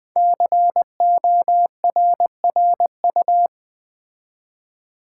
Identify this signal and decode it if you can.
Morse code, heard as a keyed tone, about 20 wpm, 700 Hz. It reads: CORRU
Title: Build A Word - 5 Letters At A Time - 10-Letter Words - 20wpm